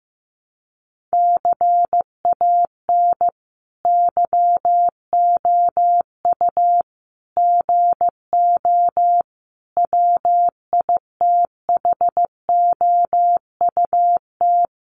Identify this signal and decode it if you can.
Morse code, heard as a keyed tone, about 15 wpm, 700 Hz. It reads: CAN YOU GO WITHOUT
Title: Sentences from Top 1500 Words 15wpm